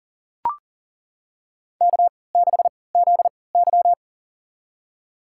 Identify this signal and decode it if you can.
Morse code, heard as a keyed tone, about 40 wpm, spaced as for 23 wpm, 700 Hz. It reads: K6ZY
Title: USA Call Signs - 4 Characters - 3x Character Spacing - 40wpm